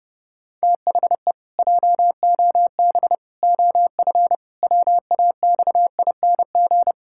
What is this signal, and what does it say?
Morse code, heard as a keyed tone, about 30 wpm, 700 Hz. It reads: THE JOB OF WAXING
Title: Pangram Challenge v2 - 30wpm